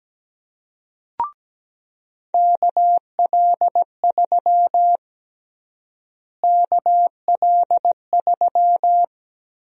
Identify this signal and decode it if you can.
Morse code, heard as a keyed tone, about 17 wpm, 700 Hz. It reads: KL3 KL3
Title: USA Call Sign Prefixes - Repeated 2x - 17wpm